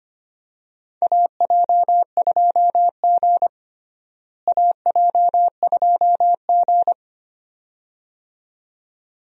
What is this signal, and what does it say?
Morse code, heard as a keyed tone, about 25 wpm, 700 Hz. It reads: AJ2G AJ2G
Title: USA Call Signs - 4 Characters - Repeated 2x - 25wpm